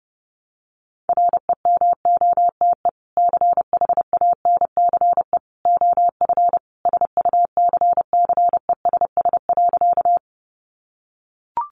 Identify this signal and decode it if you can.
Morse code, heard as a keyed tone, about 30 wpm, 700 Hz. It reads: REMOTE CHANCE OF SUCCESS.
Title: Sentences from Top 1300 Words 30wpm